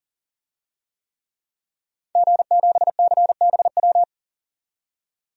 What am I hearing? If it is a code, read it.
Morse code, heard as a keyed tone, about 39 wpm, 700 Hz. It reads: G7CBW
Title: DXCC-100 Call Signs - 5 Characters - 40wpm